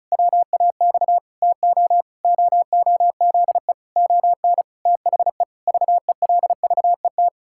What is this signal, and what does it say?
Morse code, heard as a keyed tone, about 36 wpm, 700 Hz. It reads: WAX TO OOZE ON THE VELVET
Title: Pangram Challenge v2 - 35wpm